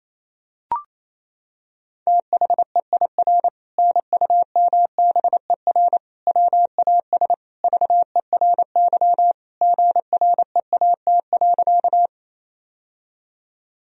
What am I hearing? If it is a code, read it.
Morse code, heard as a keyed tone, about 28 wpm, 700 Hz. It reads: THEIR NUMBER WAS VERY GREAT.
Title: Sentences from Top 1000 Words - Review Part 1 - 28wpm